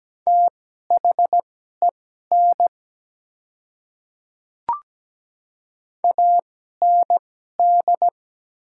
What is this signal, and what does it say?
Morse code, heard as a keyed tone, about 17 wpm, spaced as for 12 wpm, 700 Hz. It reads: THEN AND
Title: 2x Character Spacing - Top 100 Words - 17wpm